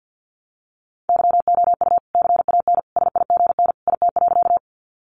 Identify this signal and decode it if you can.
Morse code, heard as a keyed tone, about 50 wpm, 700 Hz. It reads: YOU CAN SIGN IT.
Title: Sentences from Top 1000 Words 50wpm